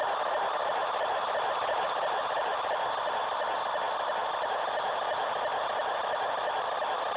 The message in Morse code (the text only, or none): none